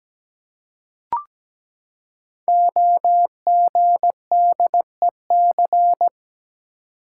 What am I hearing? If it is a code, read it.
Morse code, heard as a keyed tone, about 17 wpm, 700 Hz. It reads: OGDEC
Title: Code Groups - 5 Letters - 17wpm